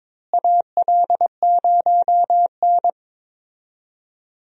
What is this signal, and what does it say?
Morse code, heard as a keyed tone, about 22 wpm, 700 Hz. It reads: AL0N
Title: USA Call Signs - 4 Characters - Repeated 3x - 22wpm